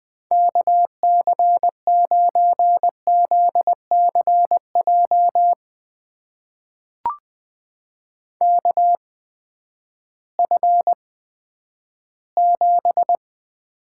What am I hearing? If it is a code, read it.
Morse code, heard as a keyed tone, about 20 wpm, spaced as for 11 wpm, 700 Hz. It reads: KC9ZCJ KF7
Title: USA Call Signs - 6 Characters - 8x Character Spacing - 20wpm